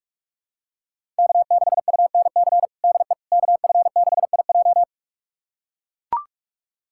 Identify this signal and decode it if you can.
Morse code, heard as a keyed tone, about 41 wpm, 700 Hz. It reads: K6UNC DE KF6IJ
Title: USA Callsign DE Callsign - 5 Characters - 45wpm